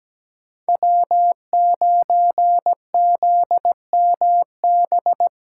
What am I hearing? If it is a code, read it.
Morse code, heard as a keyed tone, about 17 wpm, 700 Hz. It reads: W9ZMB
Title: USA Call Signs - 5 Characters - 2x Character Spacing - 17wpm